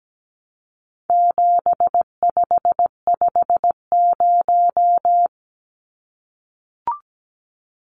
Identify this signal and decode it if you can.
Morse code, heard as a keyed tone, about 17 wpm, 700 Hz. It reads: 7550
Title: Numbers - 4-Digit 17wpm